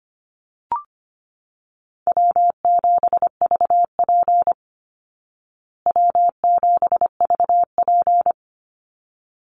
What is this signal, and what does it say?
Morse code, heard as a keyed tone, about 25 wpm, 700 Hz. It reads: W7VP W7VP
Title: Call Signs Repeated 3x - 25wpm